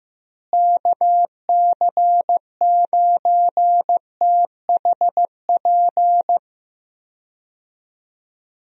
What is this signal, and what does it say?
Morse code, heard as a keyed tone, about 15 wpm, 700 Hz. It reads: KC9THP
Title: USA Call Signs - 6 Characters - Repeated 2x - 15wpm